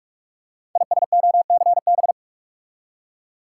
Morse code, heard as a keyed tone, about 44 wpm, 690 Hz. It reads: ISOCB
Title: Code Groups - 5 Letters - 45wpm